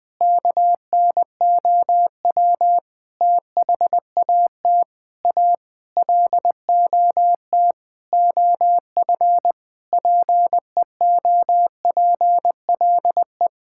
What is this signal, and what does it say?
Morse code, heard as a keyed tone, about 20 wpm, 700 Hz. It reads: KNOW THAT A LOT OF PEOPLE
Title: Sentences with 4-Word Phrases - 20wpm